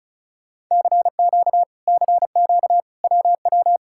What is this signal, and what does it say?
Morse code, heard as a keyed tone, about 46 wpm, 700 Hz. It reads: CQ CQ WW